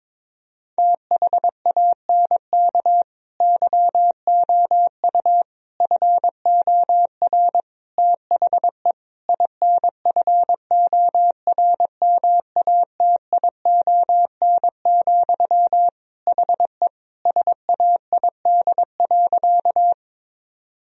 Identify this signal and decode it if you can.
Morse code, heard as a keyed tone, about 22 wpm, 700 Hz. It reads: THANK YOU FOR THE INFORMATION, HE SAID.
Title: Sentences from Top 1000 Words - Review Part 3 - 22wpm